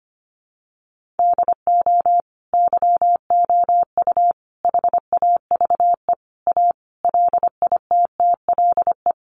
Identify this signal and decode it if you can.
Morse code, heard as a keyed tone, about 25 wpm, 700 Hz. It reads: DO YOU HAVE A LITTLE
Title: Sentences from Top 800 Words 25wpm